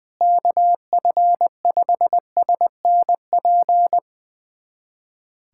Morse code, heard as a keyed tone, about 20 wpm, 700 Hz. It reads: KF5SNP
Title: USA Callsign DE Callsign - 6 Characters - 20wpm